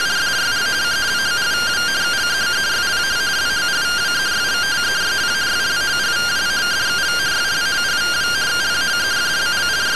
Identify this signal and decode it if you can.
Two-tone teleprinter signal, about 170 Hz shift.
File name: findings / MR AXIAL fr1 120.gen-09 rtty rx